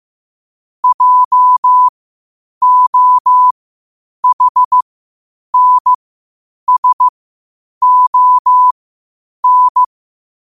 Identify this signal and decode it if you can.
Morse code, heard as a keyed tone, about 15 wpm, spaced as for 8 wpm, 1000 Hz. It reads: JOHNSON
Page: New quotes every day in morse code at 8 Words per minute.